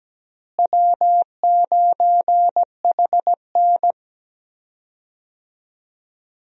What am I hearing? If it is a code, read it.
Morse code, heard as a keyed tone, about 17 wpm, 700 Hz. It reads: W9HN
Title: USA Call Signs - 4 Characters - Repeated 3x - 17wpm